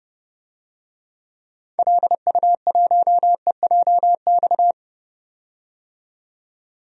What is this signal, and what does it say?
Morse code, heard as a keyed tone, about 30 wpm, 700 Hz. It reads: LU1EJX
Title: DXCC-100 Call Signs - 6 Characters - 30wpm